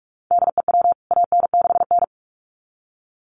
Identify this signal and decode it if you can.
Morse code, heard as a keyed tone, about 44 wpm, 700 Hz. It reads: DEW AN6N
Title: CWT Exchange 45wpm